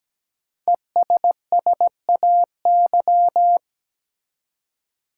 Morse code, heard as a keyed tone, about 17 wpm, 700 Hz. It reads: ESSAY
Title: Five Letter Words - 17wpm